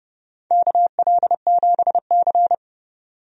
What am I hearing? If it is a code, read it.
Morse code, heard as a keyed tone, about 30 wpm, 700 Hz. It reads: KL7C
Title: DXCC-100 Call Signs - 4 Characters - 30wpm